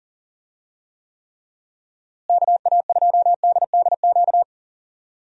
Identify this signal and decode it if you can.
Morse code, heard as a keyed tone, about 38 wpm, 700 Hz. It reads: KA2DDQ
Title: USA Call Signs - 6 Characters - Repeated 3x - 40wpm